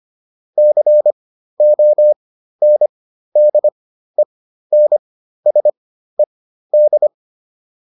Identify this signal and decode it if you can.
Morse code, heard as a keyed tone, about 25 wpm, spaced as for 13 wpm, 600 Hz. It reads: CONDENSED